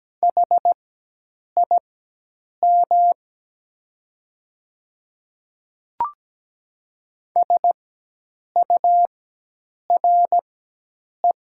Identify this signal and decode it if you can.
Morse code, heard as a keyed tone, about 17 wpm, spaced as for 8 wpm, 700 Hz. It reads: HIM SURE
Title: 4x Character Spacing - Top 300 Words - 17wpm